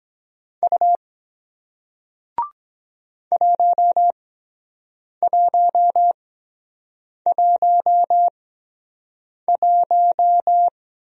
Morse code, heard as a keyed tone, about 23 wpm, spaced as for 7 wpm, 700 Hz. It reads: U 1111